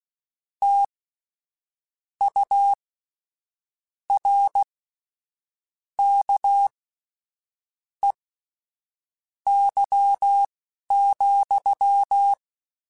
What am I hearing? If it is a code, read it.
Morse code, heard as a keyed tone, about 16 wpm, spaced as for 6 wpm, 800 Hz. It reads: TURKEY,
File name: Misc / Misc-mp3s / WordPairs sp @10